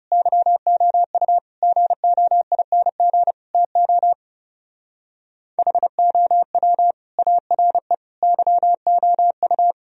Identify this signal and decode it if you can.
Morse code, heard as a keyed tone, about 37 wpm, 700 Hz. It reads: YOU GOING TO HOW ARE YOU
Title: Speed-Racing - Sets of 5 Words 30wpm